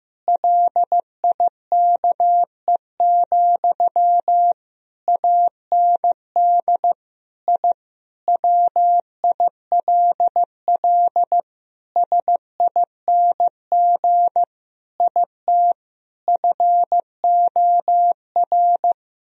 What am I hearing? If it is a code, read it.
Morse code, heard as a keyed tone, about 15 wpm, 700 Hz. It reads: LIKE, AND I WILL SING IT FOR
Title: Sentences from Top 1000 Words - Review Part 2 - 15wpm